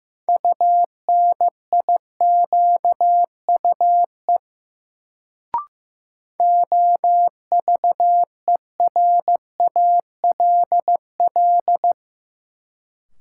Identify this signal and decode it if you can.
Morse code, heard as a keyed tone, about 16 wpm, 700 Hz. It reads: UNIQUE OVERALL
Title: Sentences from Top 1000 Words - Review Part 7 - New Word Drill - 15wpm